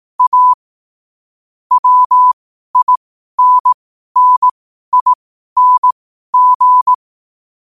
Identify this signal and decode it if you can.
Morse code, heard as a keyed tone, about 18 wpm, spaced as for 12 wpm, 1000 Hz. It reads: A WINNING
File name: qotdfiles / 12wpm